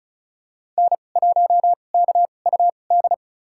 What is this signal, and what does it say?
Morse code, heard as a keyed tone, about 34 wpm, spaced as for 25 wpm, 700 Hz. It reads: N1KUD
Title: USA Call Signs - 5 Characters - 2x Character Spacing - 35wpm